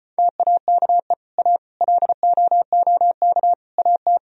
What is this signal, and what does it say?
Morse code, heard as a keyed tone, about 33 wpm, 700 Hz. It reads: TAKE A LOOK AT